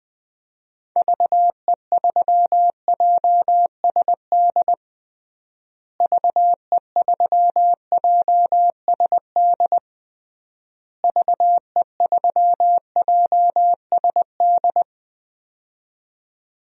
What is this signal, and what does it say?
Morse code, heard as a keyed tone, about 20 wpm, 700 Hz. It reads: VE3JSD VE3JSD VE3JSD
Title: DXCC-100 Call Signs - 6 Characters - Repeated 3x - 20wpm